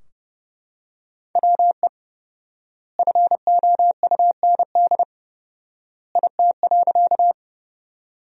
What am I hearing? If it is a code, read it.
Morse code, heard as a keyed tone, about 30 wpm, 700 Hz. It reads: WE FOUND IT.
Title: 4x Word-Spacing - Sentences from Top 300 Words - 30wpm